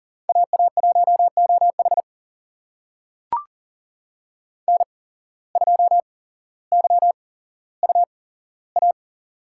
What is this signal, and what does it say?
Morse code, heard as a keyed tone, about 40 wpm, spaced as for 17 wpm, 700 Hz. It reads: AA1OH N2YUA
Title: USA Call Signs - 5 Characters - 8x Character Spacing - 40wpm